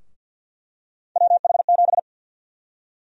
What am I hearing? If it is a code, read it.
Morse code, heard as a keyed tone, about 52 wpm, 700 Hz. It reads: WH7
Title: USA Call Sign Prefixes - Repeated 2x - 50wpm